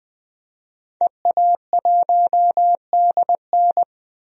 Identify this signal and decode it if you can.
Morse code, heard as a keyed tone, about 20 wpm, 700 Hz. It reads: EA1DN
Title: DXCC-100 Call Signs - 5 Characters - 20wpm